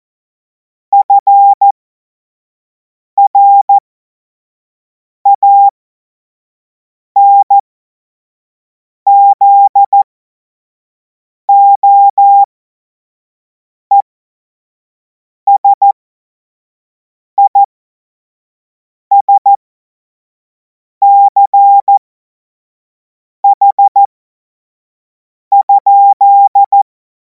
Morse code, heard as a keyed tone, about 14 wpm, spaced as for 5 wpm, 800 Hz. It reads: FRANZOESISCH?
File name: Morsekurs / Witze / Witz 21. April 2026 25 F70